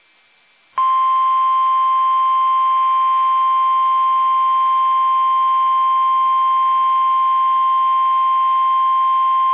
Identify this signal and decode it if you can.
Emergency Alert System (SAME) tones (the 1050 Hz attention tone). It is heard from 0.7 to 9.0 s.